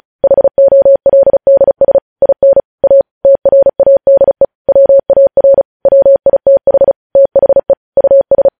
Morse code, heard as a keyed tone, about 36 wpm, 550 Hz. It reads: HOLDS IN A TRADE WAR WITH THE US
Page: news headlines in morse code at 35 wpm